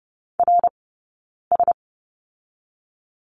Morse code, heard as a keyed tone, about 30 wpm, 700 Hz. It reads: R S
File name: Lesson 18d - Letter m - Two Characters 30wpm